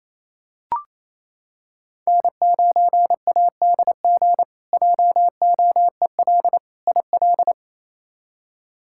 Morse code, heard as a keyed tone, about 27 wpm, 700 Hz. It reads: N9ADG JOEL IL
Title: K1USN SST QSO Practice - 28wpm